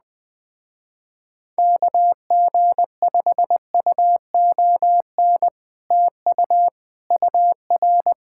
Morse code, heard as a keyed tone, about 20 wpm, 700 Hz. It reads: KG5UON TU UR
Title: Speed-Racing Common QSO Phrases - 17wpm